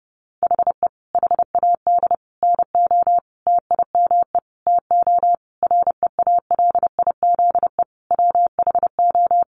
Morse code, heard as a keyed tone, about 30 wpm, 700 Hz. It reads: HE HAD NO TIME TO REALIZE WHO
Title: Sentences from Top 1000 Words 30wpm